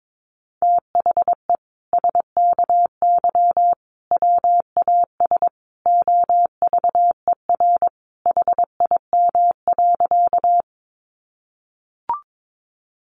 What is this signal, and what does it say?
Morse code, heard as a keyed tone, about 22 wpm, 700 Hz. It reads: THE SKY WAS OVER HIM.
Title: Sentences from Top 1200 Words 22wpm